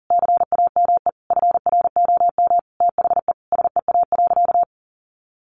Morse code, heard as a keyed tone, about 40 wpm, 700 Hz. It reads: CAME FROM THE SEA.
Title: Sentences from Top 1400 Words 40wpm